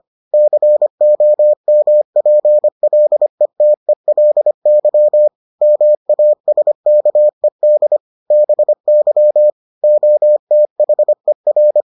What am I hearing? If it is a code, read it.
Morse code, heard as a keyed tone, about 25 wpm, 600 Hz. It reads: COMPLETELY MASKED BY OTHER